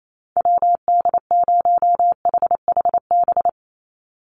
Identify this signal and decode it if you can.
Morse code, heard as a keyed tone, about 28 wpm, 700 Hz. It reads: WD0HHB
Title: USA Call Signs 28wpm